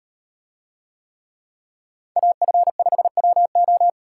Morse code, heard as a keyed tone, about 38 wpm, 700 Hz. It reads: AF5WO